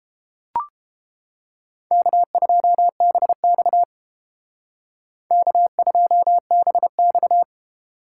Morse code, heard as a keyed tone, about 29 wpm, 700 Hz. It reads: K2BX K2BX